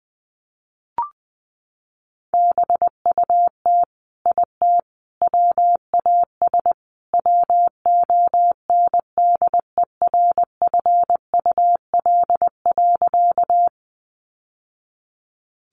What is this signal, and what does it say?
Morse code, heard as a keyed tone, about 20 wpm, 700 Hz. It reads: BUT IT WAS WONDERFUL.
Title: Sentences from Top 1100 Words 20wpm